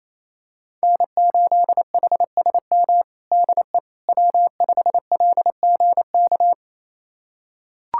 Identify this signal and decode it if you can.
Morse code, heard as a keyed tone, about 28 wpm, 700 Hz. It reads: N8HSM DE W5LGK
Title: USA Callsign DE Callsign - 5 Characters - 28wpm